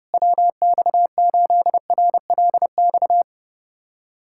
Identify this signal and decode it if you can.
Morse code, heard as a keyed tone, about 30 wpm, 700 Hz. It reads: WX8RLX
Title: USA Callsign DE Callsign 30wpm